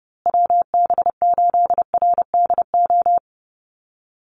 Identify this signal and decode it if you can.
Morse code, heard as a keyed tone, about 30 wpm, 700 Hz. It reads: WB8RDO
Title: USA Call Signs Repeated 2x 30wpm